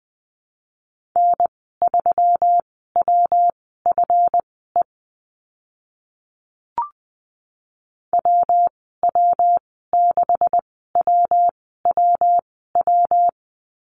Morse code, heard as a keyed tone, about 20 wpm, spaced as for 14 wpm, 700 Hz. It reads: N3WFE WW6WWW
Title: 140 Lesson 23g - Letter f - Call Signs 20wpm